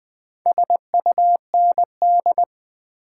Morse code, heard as a keyed tone, about 20 wpm, 700 Hz. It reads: SUND
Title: Build A Word - 2 Letters At A Time - 6-Letter Words - 20wpm